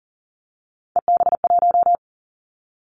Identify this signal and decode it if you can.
Morse code, heard as a keyed tone, about 41 wpm, 700 Hz. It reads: EB1
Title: Three Challenging Letter-Number Combinations 40wpm